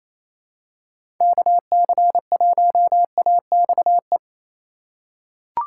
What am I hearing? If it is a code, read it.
Morse code, heard as a keyed tone, about 28 wpm, 700 Hz. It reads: KC1AXE
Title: USA Call Signs - 6 Characters - 2x Character Spacing - 28wpm